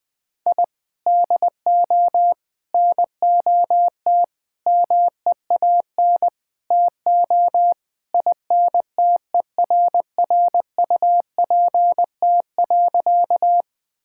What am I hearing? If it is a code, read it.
Morse code, heard as a keyed tone, about 21 wpm, 700 Hz. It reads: I DO NOT MEAN TO INTERRUPT.
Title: Sentences with 5-Word Phrases - Spoken Beginning - 20wpm